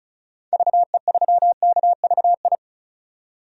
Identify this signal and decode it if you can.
Morse code, heard as a keyed tone, about 34 wpm, 700 Hz. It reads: VE3KVI
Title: DXCC-100 Call Signs - 6 Characters - 35wpm